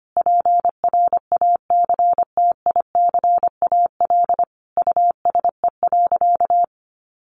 Morse code, heard as a keyed tone, about 25 wpm, 700 Hz. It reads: PRACTICAL USE.
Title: Sentences from Top 1400 Words 25wpm